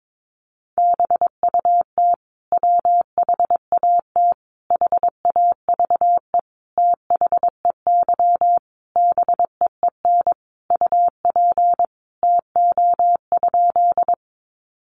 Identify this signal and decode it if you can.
Morse code, heard as a keyed tone, about 22 wpm, 700 Hz. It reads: BUT WHAT HAVE THEY BEEN UP TO?
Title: Sentences from Top 100 Words 22wpm